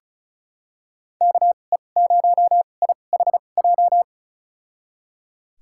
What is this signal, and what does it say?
Morse code, heard as a keyed tone, about 36 wpm, spaced as for 25 wpm, 700 Hz. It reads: KE0IHJ
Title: USA Call Signs - 6 Characters - 2x Character Spacing - 35wpm